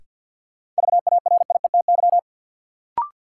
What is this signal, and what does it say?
Morse code, heard as a keyed tone, about 51 wpm, 700 Hz. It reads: VARIETY
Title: Sentences from Top 1000 Words - Review Part 4 - New Word Drill - 50wpm